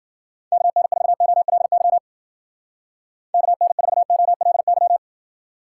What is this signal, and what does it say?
Morse code, heard as a keyed tone, about 53 wpm, 700 Hz. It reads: KN4OLY KN4OLY